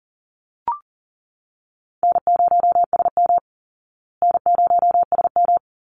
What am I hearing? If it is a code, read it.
Morse code, heard as a keyed tone, about 40 wpm, 700 Hz. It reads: N0SM N0SM
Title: Call Signs Repeated 2x - 40wpm